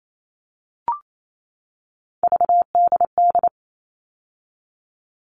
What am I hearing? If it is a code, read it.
Morse code, heard as a keyed tone, about 28 wpm, 700 Hz. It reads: VDD
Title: All Cut Numbers - 3-Digit 28wpm